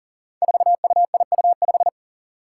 Morse code, heard as a keyed tone, about 40 wpm, 700 Hz. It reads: 4UIU5
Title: Five Challenging Letter-Number Combinations 40wpm